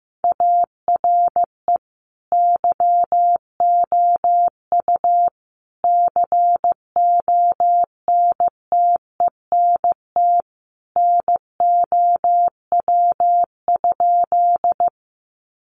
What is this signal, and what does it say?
Morse code, heard as a keyed tone, about 15 wpm, 700 Hz. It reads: ARE YOU CONTENT NOW?
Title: Sentences from Top 900 Words 15wpm